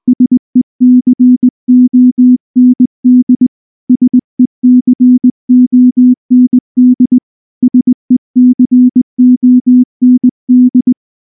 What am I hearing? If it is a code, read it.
Morse code, heard as a keyed tone, about 19 wpm, 250 Hz. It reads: SECOND SECOND SECOND
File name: morse / morse-training-20WPM-1